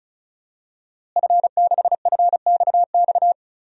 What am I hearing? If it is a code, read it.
Morse code, heard as a keyed tone, about 34 wpm, 700 Hz. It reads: F6FXX
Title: DXCC-100 Call Signs - 5 Characters - Repeated 3x - 35wpm